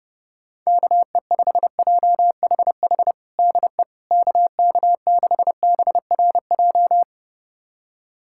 Morse code, heard as a keyed tone, about 30 wpm, 700 Hz. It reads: KE5JHH DE KK6BRJ
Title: USA Callsign DE Callsign - 6 Characters - 30wpm